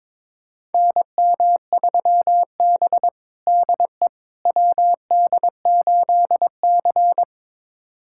Morse code, heard as a keyed tone, about 22 wpm, 700 Hz. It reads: NM3B DE WD8C
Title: USA Callsign DE Callsign - 4 Characters - 22wpm